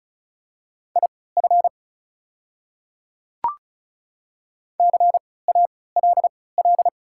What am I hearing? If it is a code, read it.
Morse code, heard as a keyed tone, about 36 wpm, spaced as for 20 wpm, 690 Hz. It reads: IF CALL
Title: 3x Character Spacing - Top 100 Words - 35wpm